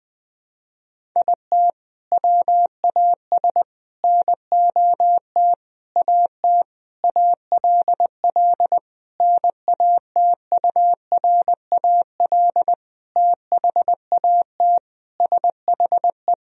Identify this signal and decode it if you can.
Morse code, heard as a keyed tone, about 20 wpm, 700 Hz. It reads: IT WAS NOT AT ALL NATURAL THAT SHE
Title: Sentences from Top 1000 Words - Review Part 4 - 20wpm